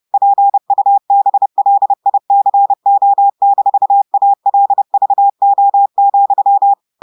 Morse code, heard as a keyed tone, about 30 wpm, 800 Hz. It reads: PUBLICO-ALVO,
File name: portugais texte en clair 2 vitesse 30 mots minutes